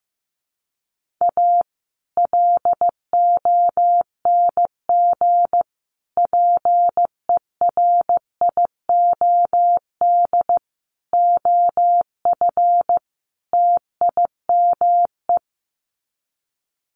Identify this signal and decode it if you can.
Morse code, heard as a keyed tone, about 15 wpm, 700 Hz. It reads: A LONG PERIOD OF TIME
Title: Sets of 5 Words 15wpm